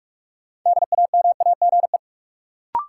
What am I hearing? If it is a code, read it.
Morse code, heard as a keyed tone, about 46 wpm, 700 Hz. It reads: DAMAGE
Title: Sentences from Top 1100 Words - New Word Drill Alt - 45wpm